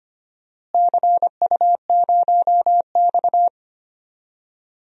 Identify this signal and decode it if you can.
Morse code, heard as a keyed tone, about 25 wpm, 700 Hz. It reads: CU0X
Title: DXCC-100 Call Signs - 4 Characters - Repeated 2x - 25wpm